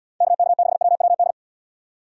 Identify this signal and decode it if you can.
Morse code, heard as a keyed tone, about 78 wpm, 700 Hz. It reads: KK6WWD